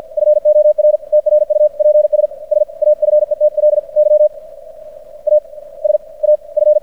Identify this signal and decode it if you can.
Morse code, heard as a keyed tone, about 50 wpm, spaced as for 38 wpm, 600 Hz. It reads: FOR TRUPS SAFETLY AS AF
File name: archived / CRSnet